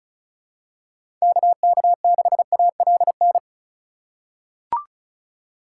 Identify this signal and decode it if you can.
Morse code, heard as a keyed tone, about 34 wpm, 700 Hz. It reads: KK6ALN
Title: USA Call Signs - 6 Characters - 2x Character Spacing - 35wpm